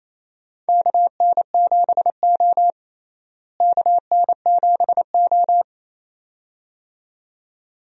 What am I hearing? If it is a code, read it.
Morse code, heard as a keyed tone, about 28 wpm, 700 Hz. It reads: KN7O KN7O
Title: USA Call Signs - 4 Characters - Repeated 2x - 28wpm